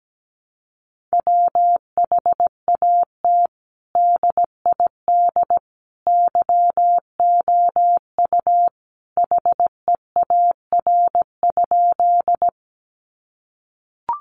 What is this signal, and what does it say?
Morse code, heard as a keyed tone, about 17 wpm, 700 Hz. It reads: WHAT DID YOU HEAR?
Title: Sentences from Top 300 Words 17wpm